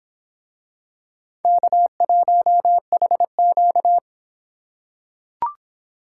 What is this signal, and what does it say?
Morse code, heard as a keyed tone, about 24 wpm, 700 Hz. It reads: K1HQ